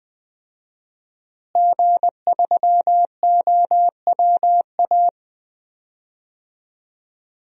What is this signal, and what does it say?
Morse code, heard as a keyed tone, about 20 wpm, 700 Hz. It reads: G3OWA
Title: DXCC-100 Call Signs - 5 Characters - Repeated 3x - 20wpm